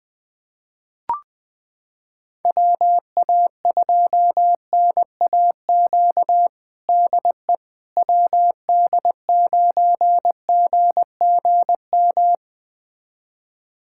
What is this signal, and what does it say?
Morse code, heard as a keyed tone, about 20 wpm, 700 Hz. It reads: WA2NAQ DE WD9GGM
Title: USA Callsign DE Callsign - 6 Characters - 20wpm